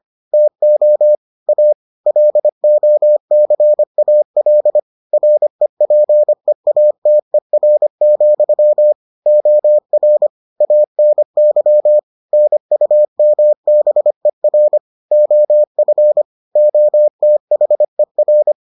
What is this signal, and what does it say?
Morse code, heard as a keyed tone, about 25 wpm, 600 Hz. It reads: TO A LOCAL REPEATER, OR ANY NUMBER OF OTHER